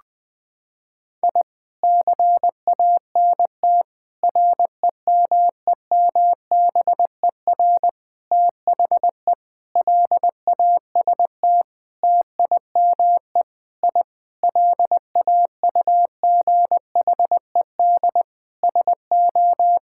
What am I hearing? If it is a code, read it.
Morse code, heard as a keyed tone, about 21 wpm, 700 Hz. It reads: I CANT REMEMBER THE LAST TIME I LAUGHED SO
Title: Sentences with 7-Word Phrases - 20wpm